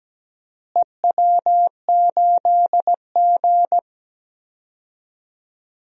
Morse code, heard as a keyed tone, about 17 wpm, 700 Hz. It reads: EW8G
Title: DXCC-100 Call Signs - 4 Characters - Repeated 3x - 17wpm